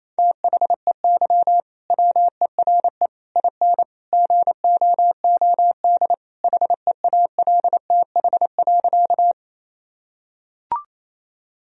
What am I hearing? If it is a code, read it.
Morse code, heard as a keyed tone, about 28 wpm, 700 Hz. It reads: THEY WERE IN GOOD HEALTH.
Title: Sentences from Top 1000 Words - Review Part 3 - 28wpm